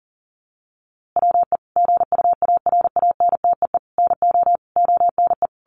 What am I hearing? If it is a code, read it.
Morse code, heard as a keyed tone, about 40 wpm, 700 Hz. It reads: WE GUARANTEE NO ONE
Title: A Sound Of Thunder 40wpm